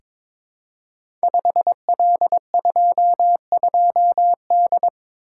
Morse code, heard as a keyed tone, about 23 wpm, 700 Hz. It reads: 5L22D
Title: Code Groups - 5 Letter:Numbers - 22wpm